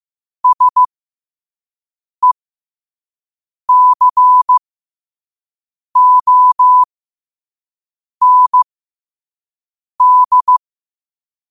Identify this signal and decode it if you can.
Morse code, heard as a keyed tone, about 15 wpm, spaced as for 5 wpm, 1000 Hz. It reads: SECOND
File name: qotdfiles / 5wpm